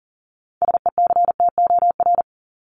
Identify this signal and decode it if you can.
Morse code, heard as a keyed tone, about 40 wpm, 700 Hz. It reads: SECTOR
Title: Top 1751-2000 Words 40wpm